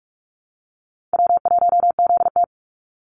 Morse code, heard as a keyed tone, about 44 wpm, 700 Hz. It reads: W1ZT
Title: Easy Call Signs Repeated 2x - 45wpm